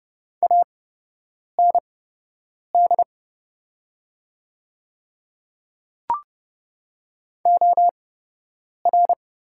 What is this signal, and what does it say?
Morse code, heard as a keyed tone, about 30 wpm, spaced as for 8 wpm, 700 Hz. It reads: AND OR
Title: 8x Character Spacing - Top 100 Words - 30wpm